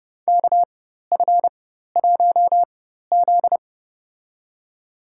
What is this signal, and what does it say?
Morse code, heard as a keyed tone, about 32 wpm, spaced as for 14 wpm, 700 Hz. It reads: KF1Z
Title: USA Call Signs - 4x Character Spacing - 30wpm